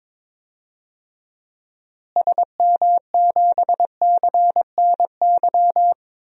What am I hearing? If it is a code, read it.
Morse code, heard as a keyed tone, about 22 wpm, 700 Hz. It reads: SM7CNY
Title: DXCC-100 Call Signs - 6 Characters - Repeated 3x - 22wpm